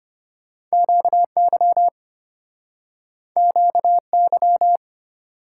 Morse code, heard as a keyed tone, about 33 wpm, 690 Hz. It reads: QY QY